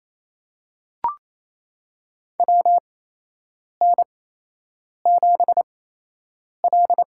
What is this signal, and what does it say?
Morse code, heard as a keyed tone, about 28 wpm, spaced as for 8 wpm, 700 Hz. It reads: WN7L
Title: USA Call Signs - 4 Characters - 8x Character Spacing - 28wpm